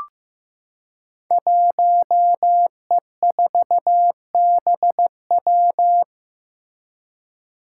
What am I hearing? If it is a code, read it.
Morse code, heard as a keyed tone, about 15 wpm, 700 Hz. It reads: 1E4BW
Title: Five Challenging Letter-Number Combinations 15wpm